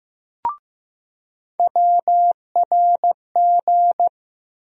Morse code, heard as a keyed tone, about 15 wpm, 700 Hz. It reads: WRG
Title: USA Call Sign Postfixes - 3 Letters 15wpm